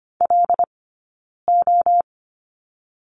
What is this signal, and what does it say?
Morse code, heard as a keyed tone, about 25 wpm, 700 Hz. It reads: L O
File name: course / 25:10wpm / 092 Lesson 16d - Letter c - Two Characters 25wpm